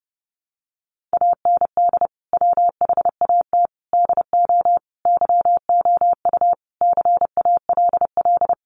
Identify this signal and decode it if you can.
Morse code, heard as a keyed tone, about 30 wpm, 700 Hz. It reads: AND WHAT DO YOU CALL
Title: Sentences from Top 100 Words 30wpm